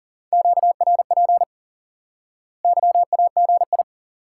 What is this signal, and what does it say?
Morse code, heard as a keyed tone, about 40 wpm, 700 Hz. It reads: QRP YAGI
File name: IQE Course 3 - Lesson 89 - YAGI - Two Elements - 40wpm